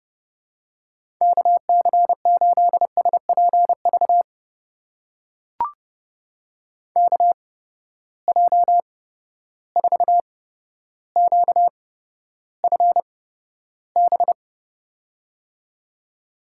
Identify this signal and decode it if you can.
Morse code, heard as a keyed tone, about 30 wpm, spaced as for 13 wpm, 700 Hz. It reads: KC8SPV KJ4QFB
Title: USA Call Signs - 6 Characters - 8x Character Spacing - 30wpm